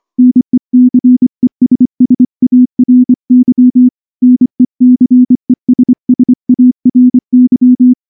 Morse code, heard as a keyed tone, about 24 wpm, 250 Hz. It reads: NECESSARY NECESSARY
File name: morse-training-25WPM